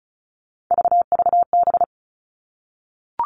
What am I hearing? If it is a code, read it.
Morse code, heard as a keyed tone, about 36 wpm, 700 Hz. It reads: VVB
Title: Three Challenging Letter-Number Combinations 35wpm